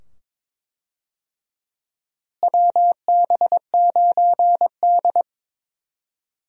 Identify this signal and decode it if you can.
Morse code, heard as a keyed tone, about 22 wpm, 700 Hz. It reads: WB9D
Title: USA Call Signs - 4 Characters - Repeated 2x - 22wpm